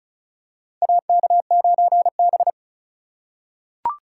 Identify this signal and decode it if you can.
Morse code, heard as a keyed tone, about 34 wpm, 700 Hz. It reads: AK9B
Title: USA Call Signs - 4 Characters - 4x Character Spacing - 35wpm